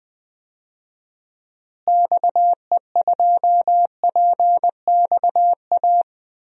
Morse code, heard as a keyed tone, about 21 wpm, 700 Hz. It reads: XE2PXA
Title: DXCC-100 Call Signs - 6 Characters - Repeated 3x - 20wpm